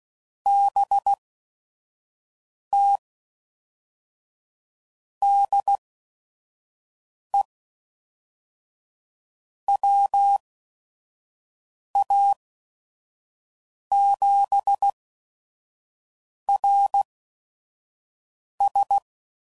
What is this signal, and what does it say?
Morse code, heard as a keyed tone, about 16 wpm, spaced as for 5 wpm, 800 Hz. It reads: BT DE WA7RS
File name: CW Code Groups @5WPM / QSOs / QSO WD0FBT